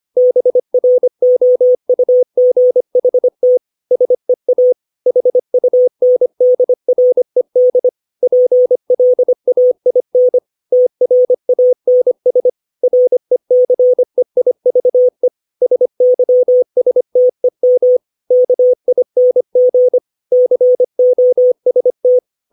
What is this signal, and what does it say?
Morse code, heard as a keyed tone, about 25 wpm, 500 Hz. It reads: BROUGHT SEA HUNDRED PLAIN TRANS RECEIVE SYSTEM KING COST
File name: 632 words / 632 Words 25WPM